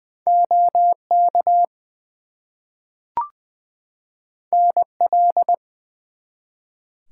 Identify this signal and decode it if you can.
Morse code, heard as a keyed tone, about 20 wpm, 700 Hz. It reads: OK NL
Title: US States and Canadian Provinces 20wpm